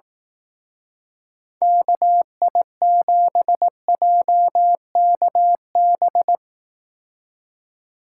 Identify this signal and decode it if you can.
Morse code, heard as a keyed tone, about 18 wpm, 700 Hz. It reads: KI7JKB
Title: Speed Racing - USA 2x3 Call Signs 15wpm